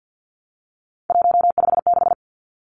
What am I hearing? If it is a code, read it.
Morse code, heard as a keyed tone, about 50 wpm, 700 Hz. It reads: 156
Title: Numbers - 3-Digit 50wpm